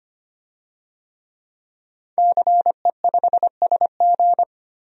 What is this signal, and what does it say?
Morse code, heard as a keyed tone, about 25 wpm, 700 Hz. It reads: CE5SG
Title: DXCC-100 Call Signs - 5 Characters - 25wpm